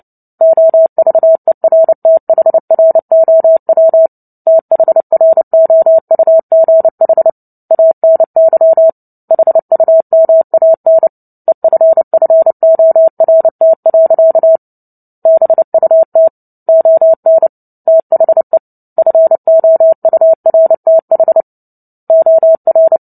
Morse code, heard as a keyed tone, about 30 wpm, 650 Hz. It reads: OVERTHROW THROUGH ANY HUMAN EFFORT. BUT ON THE FOURTH OR
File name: Book Two / 03-Chapter 3 - 29 WPM